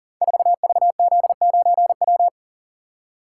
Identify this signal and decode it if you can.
Morse code, heard as a keyed tone, about 40 wpm, 690 Hz. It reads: 4VZ9W
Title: Code Groups - 5 Letter:Numbers - 40wpm